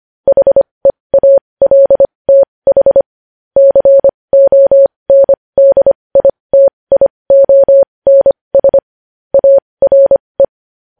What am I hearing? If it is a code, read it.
Morse code, heard as a keyed tone, about 25 wpm, spaced as for 20 wpm, 550 Hz. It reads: HEALTH CONDITIONS ARE
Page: News Headlines in Morse Code at 20 WPM